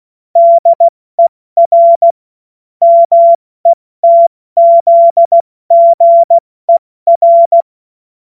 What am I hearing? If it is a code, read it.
Morse code, heard as a keyed tone, about 16 wpm, 680 Hz. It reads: DER METZGER
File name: Morsekurs / Witze / Witz 04. Oktober 2022 70 F80